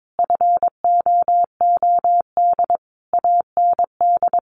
Morse code, heard as a keyed tone, about 22 wpm, 700 Hz. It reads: FOOD AND
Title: Sentences from Top 1200 Words 22wpm